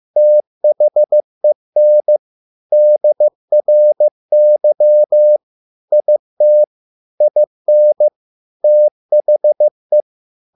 Morse code, heard as a keyed tone, about 15 wpm, 600 Hz. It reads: THEN DRY IT IN THE